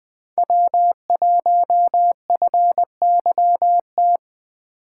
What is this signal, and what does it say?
Morse code, heard as a keyed tone, about 24 wpm, 700 Hz. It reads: W1FYT
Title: Speed Racing - USA 1x3 Call Signs 20wpm